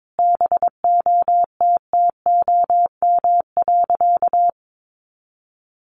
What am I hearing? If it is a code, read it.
Morse code, heard as a keyed tone, about 22 wpm, 700 Hz. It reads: BOTTOM.
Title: Easy Phrases 22wpm